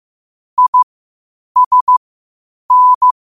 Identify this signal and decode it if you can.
Morse code, heard as a keyed tone, about 15 wpm, spaced as for 8 wpm, 1000 Hz. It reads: ISN
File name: qotdfiles / 8wpm